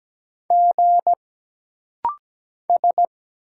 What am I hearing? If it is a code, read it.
Morse code, heard as a keyed tone, about 17 wpm, 700 Hz. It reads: G S
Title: Single Letters - Rapid-Fire 17wpm